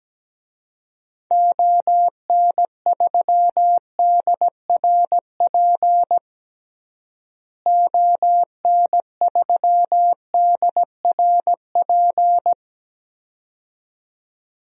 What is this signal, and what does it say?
Morse code, heard as a keyed tone, about 17 wpm, 700 Hz. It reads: ON3DRP ON3DRP
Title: DXCC-100 Call Signs - 6 Characters - Repeated 2x - 17wpm